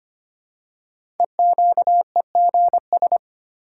Morse code, heard as a keyed tone, about 25 wpm, 700 Hz. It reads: EQEGS
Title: Code Groups - 5 Letters - 25wpm